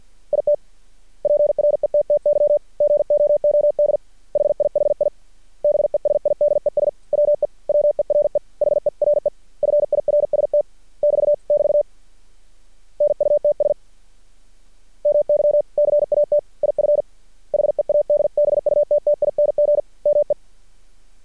Morse code, heard as a keyed tone, about 50 wpm, 590 Hz. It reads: IT PRETTY GOOD HIHI BESIDES WE WERE HERE FIRST <BT> <BT> NUTS MY CAT IF HEADBUTTING ME
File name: archived / CRSnet